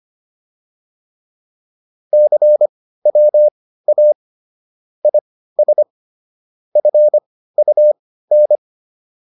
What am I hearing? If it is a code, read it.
Morse code, heard as a keyed tone, about 25 wpm, spaced as for 15 wpm, 600 Hz. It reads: CWA IS FUN